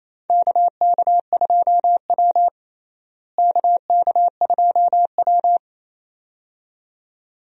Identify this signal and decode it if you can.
Morse code, heard as a keyed tone, about 28 wpm, 700 Hz. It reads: KK2W KK2W
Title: USA Call Signs - 4 Characters - Repeated 2x - 28wpm